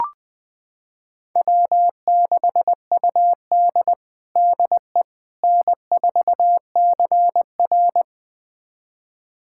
Morse code, heard as a keyed tone, about 20 wpm, 700 Hz. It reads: W6UD DE N4CR
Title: USA Callsign DE Callsign - 4 Characters - 20wpm